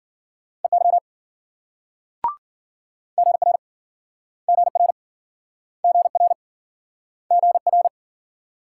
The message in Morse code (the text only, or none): EY GR GR GR GR